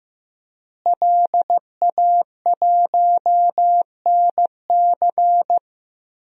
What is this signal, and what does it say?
Morse code, heard as a keyed tone, about 15 wpm, 700 Hz. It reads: LA1NC
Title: DXCC-100 Call Signs - 5 Characters - Repeated 2x - 15wpm